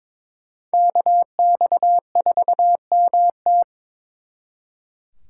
Morse code, heard as a keyed tone, about 26 wpm, 700 Hz. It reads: KX4MT
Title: Speed Racing - USA 2x2 Call Signs 22wpm